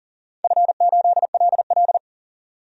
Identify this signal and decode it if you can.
Morse code, heard as a keyed tone, about 40 wpm, 700 Hz. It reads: F8LL
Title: DXCC-100 Call Signs - 4 Characters - 40wpm